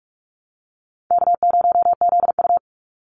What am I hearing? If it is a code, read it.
Morse code, heard as a keyed tone, about 44 wpm, 700 Hz. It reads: K0ZU
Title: Easy Call Signs Repeated 3x - 45wpm